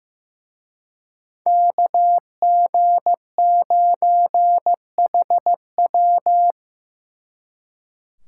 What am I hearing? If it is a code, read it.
Morse code, heard as a keyed tone, about 18 wpm, 700 Hz. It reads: KG9HW
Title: Speed Racing - USA 2x2 Call Signs 15wpm